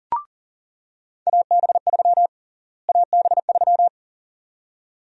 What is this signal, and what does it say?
Morse code, heard as a keyed tone, about 40 wpm, 700 Hz. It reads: AB3 AB3
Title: USA Call Sign Prefixes - Repeated 2x - 40wpm